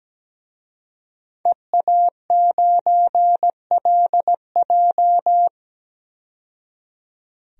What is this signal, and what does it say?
Morse code, heard as a keyed tone, about 17 wpm, 700 Hz. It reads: EA9LJ
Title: DXCC-100 Call Signs - 5 Characters - 17wpm